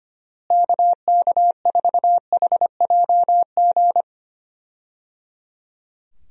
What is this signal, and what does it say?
Morse code, heard as a keyed tone, about 25 wpm, 700 Hz. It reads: KK4HJG
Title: USA Call Signs - 6 Characters - Repeated 3x - 25wpm